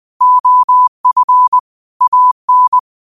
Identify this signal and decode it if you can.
Morse code, heard as a keyed tone, about 20 wpm, 1000 Hz. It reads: OF AN
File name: qotdfiles / 20wpm